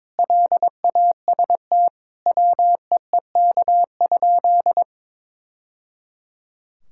Morse code, heard as a keyed tone, about 22 wpm, 700 Hz. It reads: LAST WEEK?
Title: Sentences from Top 1000 Words - Review Part 5 - 22wpm